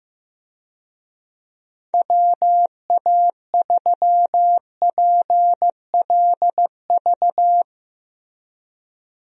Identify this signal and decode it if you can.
Morse code, heard as a keyed tone, about 15 wpm, 700 Hz. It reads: WA3PLV
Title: USA Call Signs - 6 Characters - Repeated 2x - 15wpm